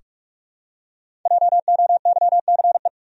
Morse code, heard as a keyed tone, about 44 wpm, 700 Hz. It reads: JOYCE
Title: USA Common Names - 45wpm